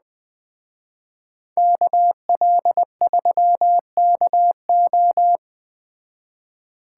Morse code, heard as a keyed tone, about 20 wpm, 700 Hz. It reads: KL3KO
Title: Speed Racing - USA 2x2 Call Signs 17wpm